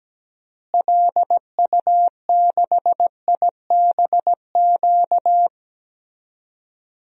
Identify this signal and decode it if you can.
Morse code, heard as a keyed tone, about 17 wpm, 700 Hz. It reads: LU6IBQ
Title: DXCC-100 Call Signs - 6 Characters - Repeated 2x - 17wpm